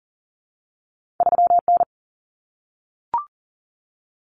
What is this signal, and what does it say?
Morse code, heard as a keyed tone, about 40 wpm, 700 Hz. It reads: 3N
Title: Common Cut Numbers - 2-Digit 40wpm